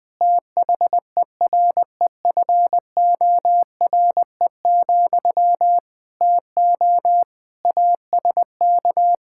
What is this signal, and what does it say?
Morse code, heard as a keyed tone, about 20 wpm, 700 Hz. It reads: THEREFORE, TO ASK
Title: Sentences from Top 1000 Words - Review Part 5 - 20wpm